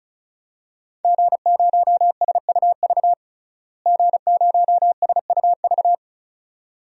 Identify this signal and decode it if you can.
Morse code, heard as a keyed tone, about 35 wpm, 700 Hz. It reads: G0SUV G0SUV
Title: DXCC-100 Call Signs - 5 Characters - Repeated 2x - 35wpm